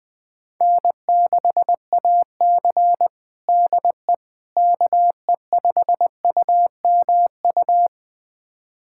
Morse code, heard as a keyed tone, about 19 wpm, 700 Hz. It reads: N6AC DE KE5UMU
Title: USA Callsign DE Callsign 20wpm